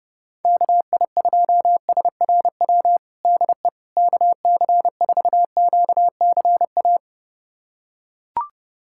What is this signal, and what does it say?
Morse code, heard as a keyed tone, about 30 wpm, 700 Hz. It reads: KI2SRW DE KC4QCA
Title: USA Callsign DE Callsign 30wpm